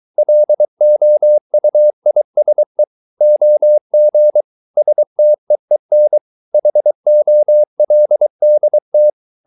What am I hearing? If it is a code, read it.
Morse code, heard as a keyed tone, about 23 wpm, 600 Hz. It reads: LOUISE OG STEEN HOLDT
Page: Ferie 23 wpm